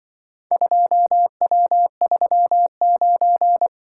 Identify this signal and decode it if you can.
Morse code, heard as a keyed tone, about 24 wpm, 700 Hz. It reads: 2W39
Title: Speed-Racing - Four Letter-Number Combinations - 20wpm